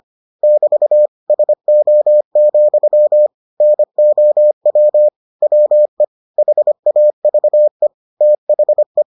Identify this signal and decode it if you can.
Morse code, heard as a keyed tone, about 25 wpm, 600 Hz. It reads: <BT> SO, NOW WE HAVE THE